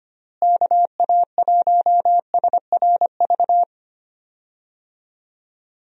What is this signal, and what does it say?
Morse code, heard as a keyed tone, about 25 wpm, 700 Hz. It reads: KA1SRV
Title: USA Call Signs - 6 Characters - Repeated 3x - 25wpm